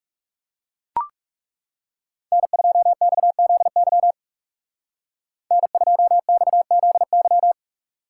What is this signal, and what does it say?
Morse code, heard as a keyed tone, about 36 wpm, 700 Hz. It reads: N2XZY N2XZY
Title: Speed Racing - USA 1x3 Call Signs 30wpm